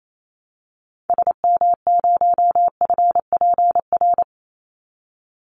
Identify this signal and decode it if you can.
Morse code, heard as a keyed tone, about 28 wpm, 700 Hz. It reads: SM0FPR
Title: Call Signs Repeated 2x - 28wpm